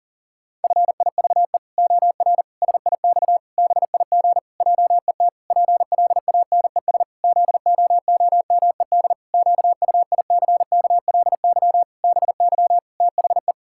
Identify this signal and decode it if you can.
Morse code, heard as a keyed tone, about 40 wpm, 700 Hz. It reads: FIVE OR SIX BIG JET PLANES ZOOMED QUICKLY BY THE
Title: Pangram Challenge v2 - 40wpm